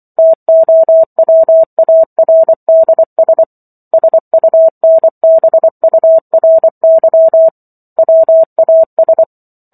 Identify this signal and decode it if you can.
Morse code, heard as a keyed tone, about 24 wpm, 650 Hz. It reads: TOWARDS SUNBURY WAS
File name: Book One / 15-Chapter 15 - 24 WPM